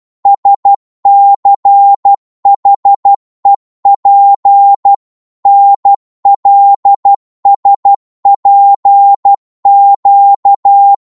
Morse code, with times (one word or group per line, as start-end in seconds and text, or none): 0.3-4.9 SCHEP
5.5-10.9 NLSPQ